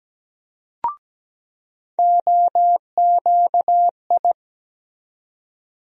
Morse code, heard as a keyed tone, about 17 wpm, 700 Hz. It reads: OQI
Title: USA Call Sign Postfixes - 3 Letters 17wpm